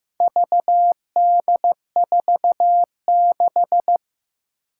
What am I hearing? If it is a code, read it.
Morse code, heard as a keyed tone, about 15 wpm, 700 Hz. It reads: VD46
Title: Four Challenging Letter-Number Combinations 15wpm